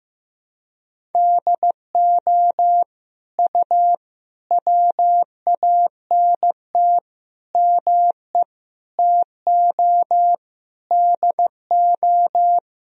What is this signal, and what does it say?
Morse code, heard as a keyed tone, about 15 wpm, 700 Hz. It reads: DO U WANT ME TO DO
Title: Sentences from Top 200 Words and Abbreviations 15wpm